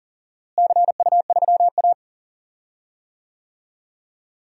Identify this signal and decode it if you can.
Morse code, heard as a keyed tone, about 40 wpm, 700 Hz. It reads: CU3A
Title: DXCC-100 Call Signs - 4 Characters - Repeated 2x - 40wpm